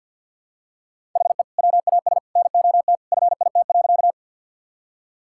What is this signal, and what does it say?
Morse code, heard as a keyed tone, about 46 wpm, 700 Hz. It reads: HE WAS NOT FIT.
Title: Sentences from Top 1000 Words - Review Part 5 - 50wpm